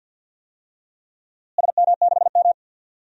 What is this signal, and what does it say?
Morse code, heard as a keyed tone, about 51 wpm, 700 Hz. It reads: SM6M
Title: DXCC-100 Call Signs - 4 Characters - Repeated 2x - 50wpm